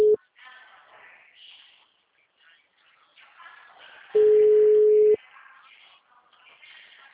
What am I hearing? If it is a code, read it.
Ringback tone (Europe, 425 Hz).